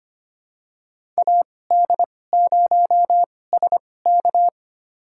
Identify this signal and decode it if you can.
Morse code, heard as a keyed tone, about 25 wpm, spaced as for 18 wpm, 700 Hz. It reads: AD0SK
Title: USA Call Signs - 5 Characters - 2x Character Spacing - 25wpm